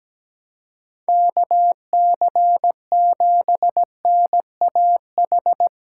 Morse code, heard as a keyed tone, about 19 wpm, 700 Hz. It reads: KC7NAH
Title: Speed Racing - USA 2x3 Call Signs 17wpm